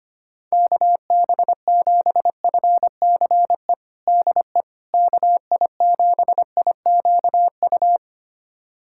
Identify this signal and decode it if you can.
Morse code, heard as a keyed tone, about 25 wpm, 700 Hz. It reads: KB7FCE DE KI7IQU
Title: USA Callsign DE Callsign - 6 Characters - 25wpm